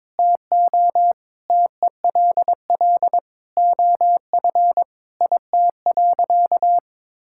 Morse code, heard as a keyed tone, about 22 wpm, 700 Hz. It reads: TO TELL OF IT.
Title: Sentences from Top 1600 Words 22wpm